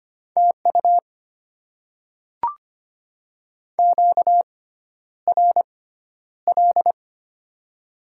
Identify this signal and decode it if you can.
Morse code, heard as a keyed tone, about 25 wpm, spaced as for 11 wpm, 690 Hz. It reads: TU QRL
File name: Top 20 QSO Elements - 6x Character Spacing - 25wpm